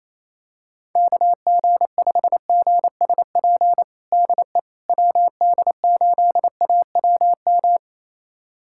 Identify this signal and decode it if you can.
Morse code, heard as a keyed tone, about 28 wpm, 700 Hz. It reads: KG5GSP DE WD8AWM
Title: USA Callsign DE Callsign 28wpm